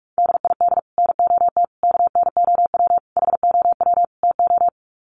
Morse code, heard as a keyed tone, about 44 wpm, 700 Hz. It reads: DID NOT KNOW HOW TO
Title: Sets of 5 Words 45wpm